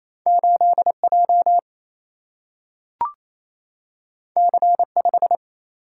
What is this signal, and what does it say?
Morse code, heard as a keyed tone, about 28 wpm, 700 Hz. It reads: 8J C5
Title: Single-Letter Single-Number 28wpm